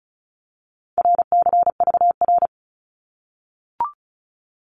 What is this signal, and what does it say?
Morse code, heard as a keyed tone, about 36 wpm, 700 Hz. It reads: RCVR
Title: Abbreviations 35wpm